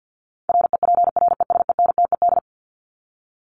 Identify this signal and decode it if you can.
Morse code, heard as a keyed tone, about 50 wpm, 700 Hz. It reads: REPRESENTED
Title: ED Suffix - Top 100 Words with Base Words 50wpm